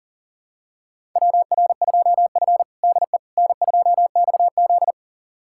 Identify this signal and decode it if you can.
Morse code, heard as a keyed tone, about 40 wpm, 700 Hz. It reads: WR2F DE N2XZ
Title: USA Callsign DE Callsign - 4 Characters - 40wpm